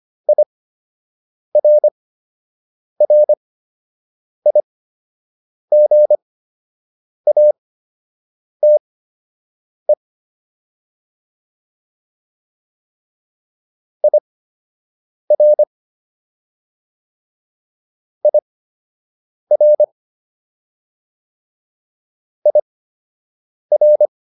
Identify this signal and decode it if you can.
Morse code, heard as a keyed tone, about 25 wpm, spaced as for 7 wpm, 600 Hz. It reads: IRRIGATE IR IR IR